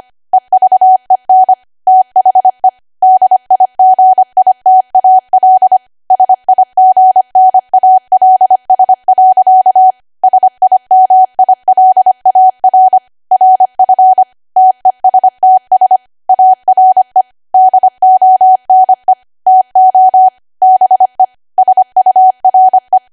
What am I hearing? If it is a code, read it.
Morse code, heard as a keyed tone, about 25 wpm, 750 Hz. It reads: EVEN THE DIGITAL SIGNALS. SIMILAR RF TESTS ARE DONE TO BE SURE
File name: Archive / 25 WPM / 25WPM